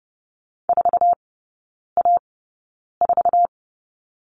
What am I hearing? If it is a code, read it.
Morse code, heard as a keyed tone, about 30 wpm, spaced as for 9 wpm, 700 Hz. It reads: 4A4
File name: Lesson 18e - Letter m - Three Characters 30wpm